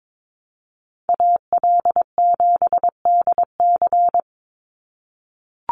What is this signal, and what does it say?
Morse code, heard as a keyed tone, about 22 wpm, 700 Hz. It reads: AL7DC
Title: USA Call Signs 22wpm